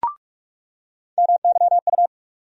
Morse code, heard as a keyed tone, about 44 wpm, 700 Hz. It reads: MYU
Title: USA Call Sign Postfixes - 3 Letters 45wpm